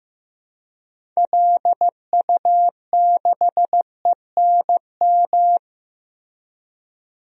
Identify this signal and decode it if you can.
Morse code, heard as a keyed tone, about 15 wpm, 700 Hz. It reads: LU6ENM
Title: DXCC-100 Call Signs - 6 Characters - Repeated 3x - 15wpm